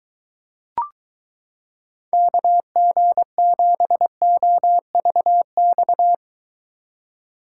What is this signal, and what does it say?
Morse code, heard as a keyed tone, about 22 wpm, 700 Hz. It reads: KG7OVX